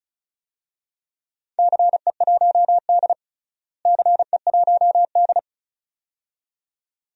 Morse code, heard as a keyed tone, about 36 wpm, 700 Hz. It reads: CE1D CE1D
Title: DXCC-100 Call Signs - 4 Characters - Repeated 2x - 35wpm